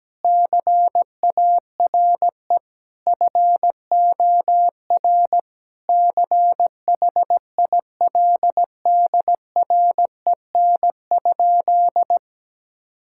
Morse code, heard as a keyed tone, about 17 wpm, 700 Hz. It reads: CARE FOR CHILDREN?
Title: Sentences from Top 1000 Words - Review Part 5 - 17wpm